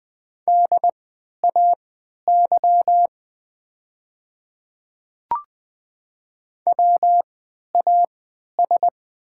Morse code, heard as a keyed tone, about 20 wpm, spaced as for 11 wpm, 700 Hz. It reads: DAY WAS
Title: 3x Character Spacing - Top 100 Words - 20wpm